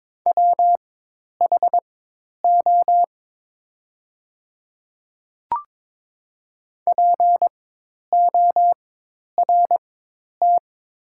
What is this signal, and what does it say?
Morse code, heard as a keyed tone, about 22 wpm, spaced as for 10 wpm, 690 Hz. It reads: WHO PORT
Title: 4x Character Spacing - Top 200 Words - 22wpm